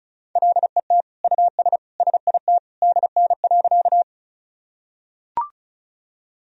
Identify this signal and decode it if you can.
Morse code, heard as a keyed tone, about 36 wpm, 700 Hz. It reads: LET US SIT DN.
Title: Sentences from Top 500 Words and Abbreviations 35wpm